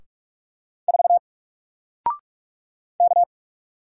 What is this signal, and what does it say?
Morse code, heard as a keyed tone, about 44 wpm, 700 Hz. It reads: 4 K
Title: Single Letter-Number Alt 45wpm